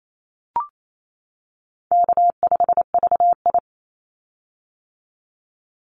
Morse code, heard as a keyed tone, about 28 wpm, 700 Hz. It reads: K5VI
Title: Easy Call Signs - 28wpm